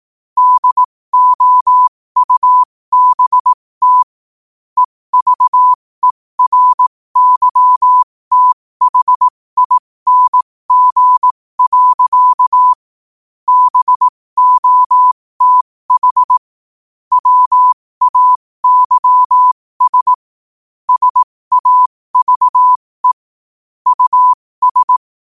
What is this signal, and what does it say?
Morse code, heard as a keyed tone, about 18 wpm, spaced as for 15 wpm, 1000 Hz. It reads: DOUBT EVERYTHING. BOTH WAYS SAVE US